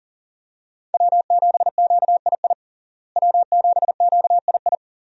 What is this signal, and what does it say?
Morse code, heard as a keyed tone, about 40 wpm, 700 Hz. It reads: W7QII W7QII
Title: USA Call Signs - 5 Characters - Repeated 2x - 40wpm